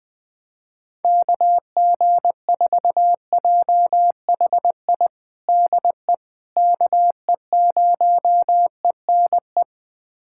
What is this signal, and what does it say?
Morse code, heard as a keyed tone, about 20 wpm, 700 Hz. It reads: KG4JHI DE KE0ENE
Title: USA Callsign DE Callsign 20wpm